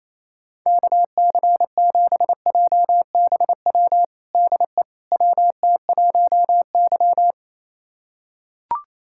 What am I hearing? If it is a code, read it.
Morse code, heard as a keyed tone, about 27 wpm, 700 Hz. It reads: KC7JBW DE WT1Y
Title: USA Callsign DE Callsign 28wpm